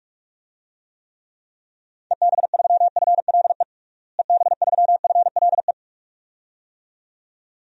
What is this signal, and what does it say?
Morse code, heard as a keyed tone, about 46 wpm, 700 Hz. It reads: EB3FLE EB3FLE
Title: DXCC-100 Call Signs - 6 Characters - Repeated 2x - 45wpm